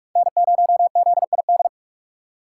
Morse code, heard as a keyed tone, about 43 wpm, 700 Hz. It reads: N0ZID